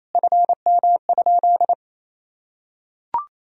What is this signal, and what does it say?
Morse code, heard as a keyed tone, about 28 wpm, 700 Hz. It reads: FM?
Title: Sentences from Top 200 Words and Abbreviations 28wpm